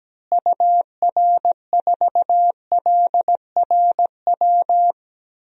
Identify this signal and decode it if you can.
Morse code, heard as a keyed tone, about 17 wpm, 700 Hz. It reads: UR4LRW
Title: DXCC-100 Call Signs - 6 Characters - Repeated 3x - 17wpm